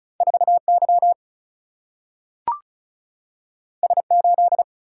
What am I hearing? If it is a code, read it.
Morse code, heard as a keyed tone, about 36 wpm, 700 Hz. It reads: 4Y S8
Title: Single-Letter Single-Number 35wpm